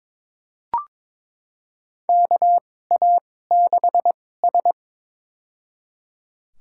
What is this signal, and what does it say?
Morse code, heard as a keyed tone, about 23 wpm, spaced as for 16 wpm, 700 Hz. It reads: KA6S
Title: USA Call Signs - 4 Characters - 2x Character Spacing - 22wpm